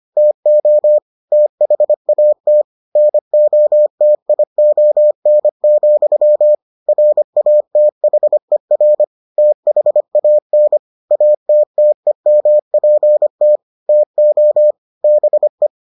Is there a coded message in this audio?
Yes: Morse code from 0.2 to 15.7 s, heard as a keyed tone, about 25 wpm, 600 Hz. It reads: TO THAT NOTION, RATHER THAN ATTEMPT TO BE